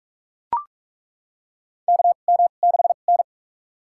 Morse code, heard as a keyed tone, about 41 wpm, spaced as for 31 wpm, 700 Hz. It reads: KM6N
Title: USA Call Signs - 4 Characters - 2x Character Spacing - 45wpm